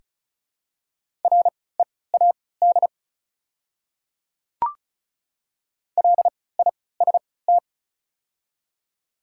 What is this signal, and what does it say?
Morse code, heard as a keyed tone, about 36 wpm, spaced as for 20 wpm, 700 Hz. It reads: READ LIST
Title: 3x Character Spacing - Top 500 Words - 35wpm